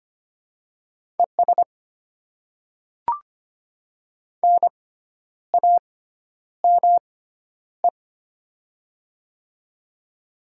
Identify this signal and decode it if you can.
Morse code, heard as a keyed tone, about 25 wpm, spaced as for 10 wpm, 700 Hz. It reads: ES NAME
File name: Top 20 QSO Elements - 6x Character Spacing - 25wpm